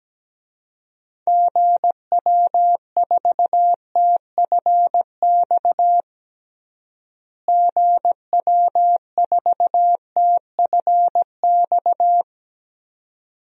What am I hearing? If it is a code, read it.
Morse code, heard as a keyed tone, about 17 wpm, 700 Hz. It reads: GW4TFX GW4TFX
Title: DXCC-100 Call Signs - 6 Characters - Repeated 2x - 17wpm